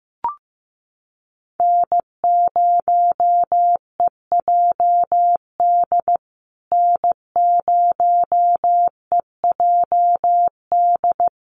Morse code, heard as a keyed tone, about 15 wpm, 700 Hz. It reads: N0EJD N0EJD
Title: USA Call Signs Repeated 2x 15wpm